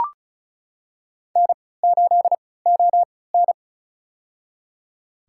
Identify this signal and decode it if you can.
Morse code, heard as a keyed tone, about 34 wpm, spaced as for 20 wpm, 700 Hz. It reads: N8ON
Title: USA Call Signs - 4 Characters - 3x Character Spacing - 35wpm